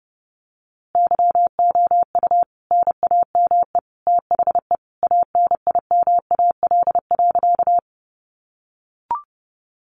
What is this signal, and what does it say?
Morse code, heard as a keyed tone, about 30 wpm, 700 Hz. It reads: YOU NAME THE ANIMAL.
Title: A Sound Of Thunder 30wpm